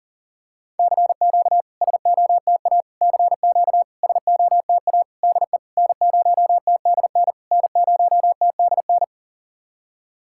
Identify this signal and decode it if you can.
Morse code, heard as a keyed tone, about 40 wpm, 660 Hz. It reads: CQ SOTA CQ SOTA DE N0TDN N0TDN
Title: SOTA QSO Practice 40wpm